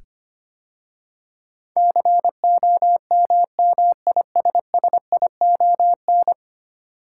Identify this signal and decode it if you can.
Morse code, heard as a keyed tone, about 25 wpm, 700 Hz. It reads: COMMISSION
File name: 10 Letter Words - First 3 Letters Given - 25wpm